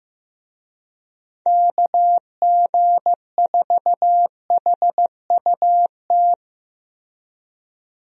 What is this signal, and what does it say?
Morse code, heard as a keyed tone, about 15 wpm, 700 Hz. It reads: KG4HUT
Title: USA Call Signs - 6 Characters - Repeated 3x - 15wpm